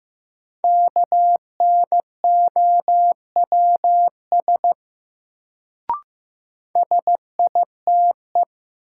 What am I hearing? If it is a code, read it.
Morse code, heard as a keyed tone, about 15 wpm, 700 Hz. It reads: KNOWS SITE
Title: Sentences from Top 1000 Words - Review Part 1 - New Word Drill - 15wpm